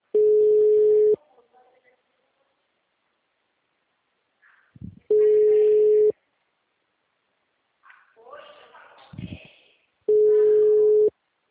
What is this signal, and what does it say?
Ringback tone (Europe, 425 Hz), with speech.
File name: out- -2029- - - .34739